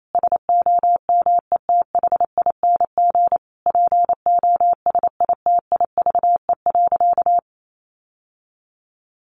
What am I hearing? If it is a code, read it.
Morse code, heard as a keyed tone, about 28 wpm, 700 Hz. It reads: SOMETHING POSITIVE.
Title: Sentences from Top 1400 Words 28wpm